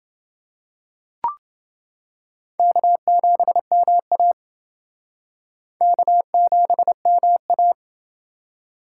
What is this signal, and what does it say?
Morse code, heard as a keyed tone, about 28 wpm, 700 Hz. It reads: K7MA K7MA